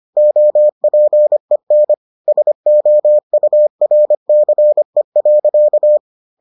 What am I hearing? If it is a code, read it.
Morse code, heard as a keyed tone, about 25 wpm, 600 Hz. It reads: OPEN SOURCE.